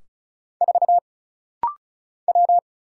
Morse code, heard as a keyed tone, about 36 wpm, 700 Hz. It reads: 4 W
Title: Single Letter-Number Alt - Rapid-Fire 35wpm